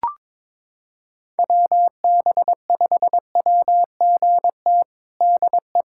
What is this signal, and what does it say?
Morse code, heard as a keyed tone, about 22 wpm, 700 Hz. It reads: WB5WGT DE
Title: USA Callsign DE Callsign 22wpm